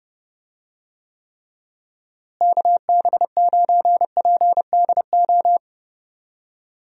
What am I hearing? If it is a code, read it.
Morse code, heard as a keyed tone, about 30 wpm, 700 Hz. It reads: KB9PDO
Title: USA Call Signs - 6 Characters - Repeated 3x - 30wpm